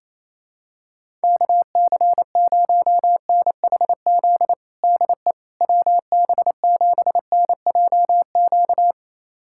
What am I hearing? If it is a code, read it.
Morse code, heard as a keyed tone, about 28 wpm, 700 Hz. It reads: KC0NHZ DE WB7NJQ
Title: USA Callsign DE Callsign - 6 Characters - 28wpm